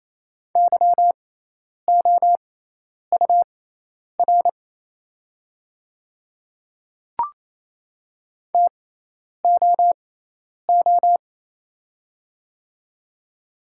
Morse code, heard as a keyed tone, about 27 wpm, spaced as for 10 wpm, 700 Hz. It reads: YOUR TOO
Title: 6x Character Spacing - Top 200 Words - 28wpm